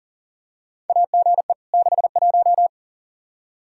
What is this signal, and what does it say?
Morse code, heard as a keyed tone, about 40 wpm, 700 Hz. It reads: AGE 61
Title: K1USN SST QSO Practice 40wpm